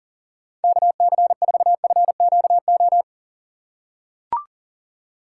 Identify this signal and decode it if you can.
Morse code, heard as a keyed tone, about 40 wpm, 700 Hz. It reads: KC4FQO
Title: USA Call Signs - 6 Characters - 4x Character Spacing - 40wpm